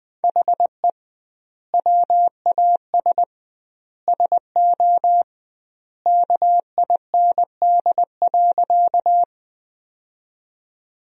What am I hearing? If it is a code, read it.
Morse code, heard as a keyed tone, about 21 wpm, 700 Hz. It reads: HE WAS SO KIND.
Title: 2x Word-Spacing - Sentences from Top 200 Words - 20wpm